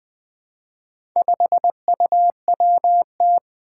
Morse code, heard as a keyed tone, about 20 wpm, 700 Hz. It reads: 5UWT
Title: Four Challenging Letter-Number Combinations 20wpm